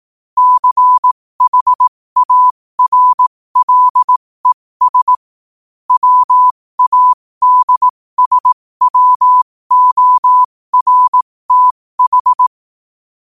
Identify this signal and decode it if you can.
Morse code, heard as a keyed tone, about 18 wpm, spaced as for 15 wpm, 1000 Hz. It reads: CHARLES WADSWORTH
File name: qotdfiles / 15wpm